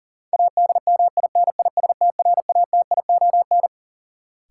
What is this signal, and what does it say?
Morse code, heard as a keyed tone, about 40 wpm, 700 Hz. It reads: ADMINISTRATION
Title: Sets of 5 Words - New Word Drill - 40wpm